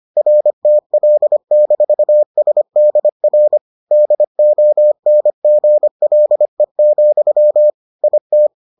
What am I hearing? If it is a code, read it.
Morse code, heard as a keyed tone, about 25 wpm, 600 Hz. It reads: RTL-SDR DONGLE, IT